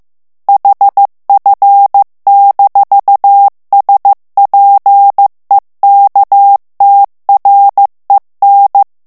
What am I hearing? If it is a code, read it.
Morse code, heard as a keyed tone, about 15 wpm, 800 Hz. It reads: HF-SPEKTREN
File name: morse mp3 / ALLEMAND texte clair 1 vitesse 14 mots minute